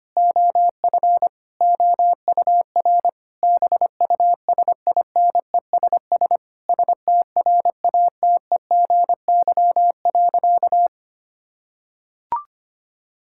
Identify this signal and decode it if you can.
Morse code, heard as a keyed tone, about 25 wpm, 700 Hz. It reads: OF OUR BUSINESS STRATEGY.
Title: Sentences from Top 1000 Words - Review Part 6 - 25wpm